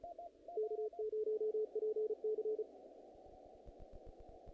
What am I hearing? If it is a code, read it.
Morse code, heard as a keyed tone, about 32 wpm, 470 Hz. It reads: K0PC
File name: contests / naqp-cw-k5tr-op-k5pi- -1900